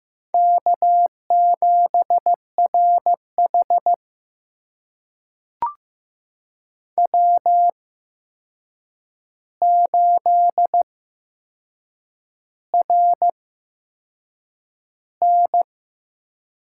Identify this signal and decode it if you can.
Morse code, heard as a keyed tone, about 15 wpm, spaced as for 6 wpm, 700 Hz. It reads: K7RH W8RN
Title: USA Call Signs - 4 Characters - 8x Character Spacing - 15wpm